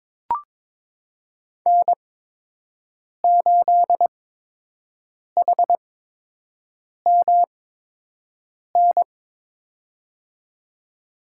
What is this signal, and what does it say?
Morse code, heard as a keyed tone, about 22 wpm, spaced as for 6 wpm, 700 Hz. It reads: N8HMN
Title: USA Call Signs - 8x Character Spacing - 22wpm